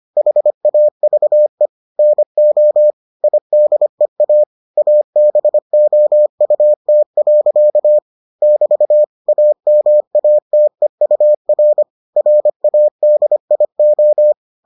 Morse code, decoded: HAVE NO IDEA ABOUT. <BT> AMATEUR RADIO

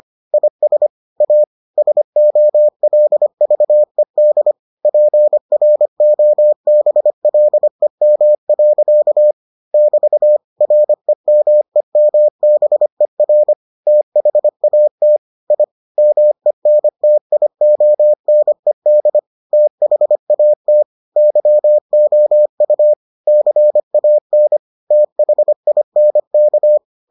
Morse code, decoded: IS A SOLVED PROBLEM. <BT> REMEMBER THAT I MENTIONED THAT YOU CAN THINK

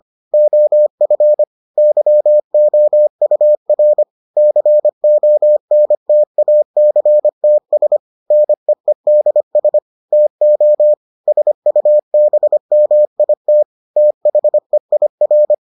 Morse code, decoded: OF YOUR CONTACTS NEEDS TO SUBMIT THEIR

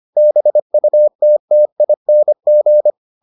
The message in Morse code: BUTTING

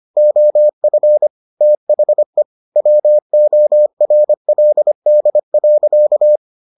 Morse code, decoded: OF THE WORLD.

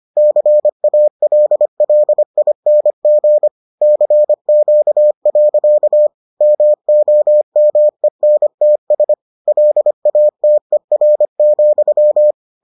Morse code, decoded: CALLING CQ. MOMENTS LATER,

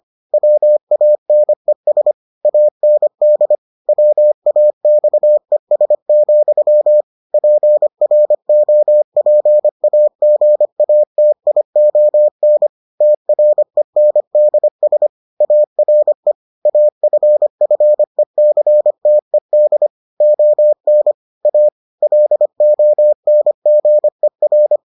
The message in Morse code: WANES AND WAXES, PROPAGATION TRENDS ARE AFFECTED ON A LONGER

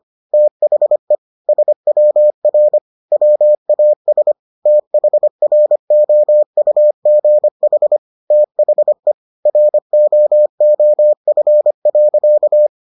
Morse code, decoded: THE SWR WAS THROUGH THE ROOF.

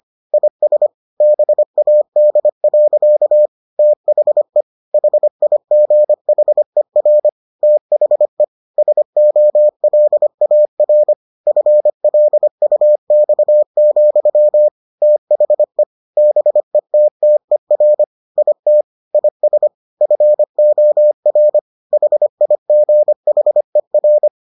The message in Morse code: IS BAD. THE HIGHER THE SOLAR FLUX, THE BETTER IT IS FOR HIGHER

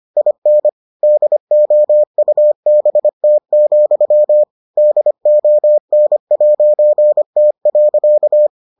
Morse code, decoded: IN DOUBT, DON'T.

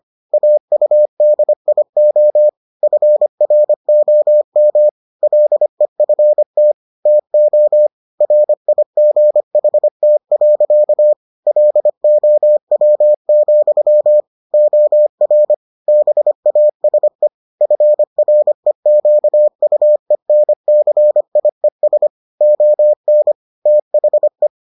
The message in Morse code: AUDIO FROM LEFT TO RIGHT. LOW, OR BASE FREQUENCIES ON THE

